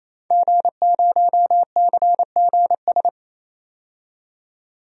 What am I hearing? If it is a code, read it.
Morse code, heard as a keyed tone, about 28 wpm, 700 Hz. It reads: G0CGS
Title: DXCC-100 Call Signs - 5 Characters - 28wpm